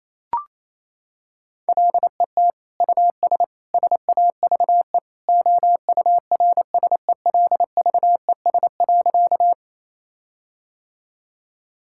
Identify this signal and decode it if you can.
Morse code, heard as a keyed tone, about 28 wpm, 700 Hz. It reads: LET US SAVE OURSELVES.
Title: Sentences from Top 1600 Words 28wpm